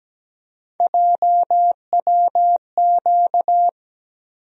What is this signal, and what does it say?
Morse code, heard as a keyed tone, about 17 wpm, 700 Hz. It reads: JWQ
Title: USA Call Sign Postfixes - 3 Letters 17wpm